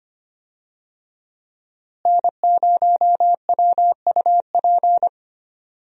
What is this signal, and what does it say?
Morse code, heard as a keyed tone, about 25 wpm, 700 Hz. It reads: N0WUP
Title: USA Call Signs - 5 Characters - Repeated 2x - 25wpm